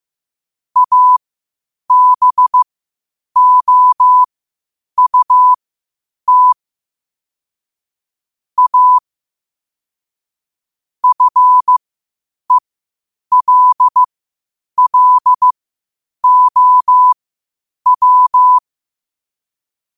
Morse code, heard as a keyed tone, about 15 wpm, spaced as for 8 wpm, 1000 Hz. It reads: ABOUT A FELLOW
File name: qotdfiles / 8wpm